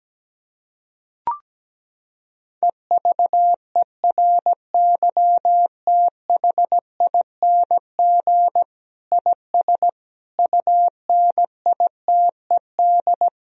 Morse code, heard as a keyed tone, about 17 wpm, 700 Hz. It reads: EVERYTHING IS UNITED
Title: Sentences from Top 1000 Words - Review Part 4 - 17wpm